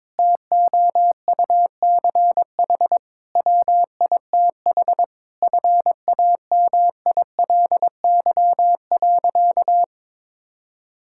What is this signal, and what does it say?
Morse code, heard as a keyed tone, about 22 wpm, 700 Hz. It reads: TOUCH WITH FAMILY.
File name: Difficult Sentences with Hints - 22wpm